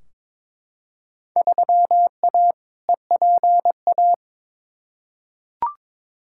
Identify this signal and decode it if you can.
Morse code, heard as a keyed tone, about 22 wpm, 700 Hz. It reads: 3A EPA
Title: Field Day Exchanges - 22wpm